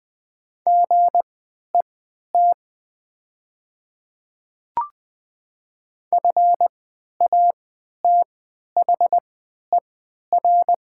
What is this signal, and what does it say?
Morse code, heard as a keyed tone, about 20 wpm, spaced as for 11 wpm, 700 Hz. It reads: GET FATHER
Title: 3x Character Spacing - Top 200 Words - 20wpm